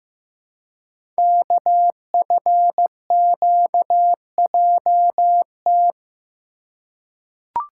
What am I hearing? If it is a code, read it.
Morse code, heard as a keyed tone, about 15 wpm, 700 Hz. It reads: KFQJT
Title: Code Groups - 5 Letters - 15wpm